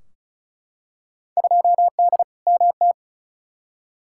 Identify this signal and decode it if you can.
Morse code, heard as a keyed tone, about 36 wpm, 700 Hz. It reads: 2D MT
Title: Field Day Exchanges Encore - 35wpm